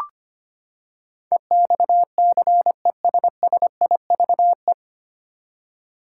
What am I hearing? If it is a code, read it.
Morse code, heard as a keyed tone, about 25 wpm, 700 Hz. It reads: EXCESSIVE
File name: Nine Letter Words - 25wpm